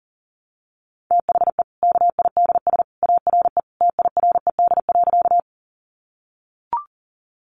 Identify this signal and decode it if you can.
Morse code, heard as a keyed tone, about 40 wpm, 700 Hz. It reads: THE KIDS ARE TIRED.
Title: Easy Phrases 40wpm